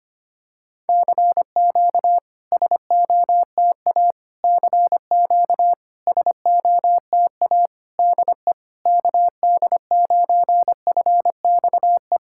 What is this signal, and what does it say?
Morse code, heard as a keyed tone, about 25 wpm, 700 Hz. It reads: CQ SOTA CQ SOTA DE KD9FXE
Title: SOTA QSO Practice 25wpm